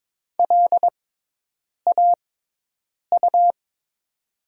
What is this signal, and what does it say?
Morse code, heard as a keyed tone, about 22 wpm, spaced as for 8 wpm, 700 Hz. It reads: LAU
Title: 6x Character Spacing - Top 500 Words - 22wpm